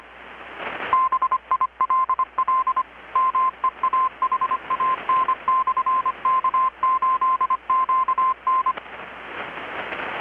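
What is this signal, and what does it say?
Morse code, heard as a keyed tone, about 25 wpm, 1040 Hz. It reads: BILL MEAHAN/K8QN
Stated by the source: Morse from KPH's Night of Nights XX maritime radiotelegraphy event, received in Chiapas, Mexico about 3,800 km away on a Kenwood R-600 with a grounded 10 m long-wire antenna.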